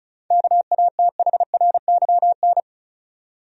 Morse code, heard as a keyed tone, about 34 wpm, 700 Hz. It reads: KATHRYN
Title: USA Common Names - 35wpm